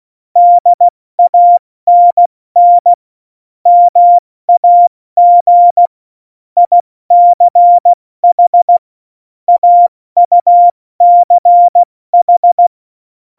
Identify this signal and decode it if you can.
Morse code, heard as a keyed tone, about 16 wpm, spaced as for 14 wpm, 700 Hz. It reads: DANN MAG ICH AUCH
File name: Morsekurs / Witze / Witz 08. Oktober 2019, 70 F80